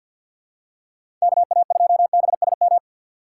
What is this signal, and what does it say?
Morse code, heard as a keyed tone, about 53 wpm, 700 Hz. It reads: KA2BSM
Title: USA Call Signs - 6 Characters - 3x Character Spacing - 50wpm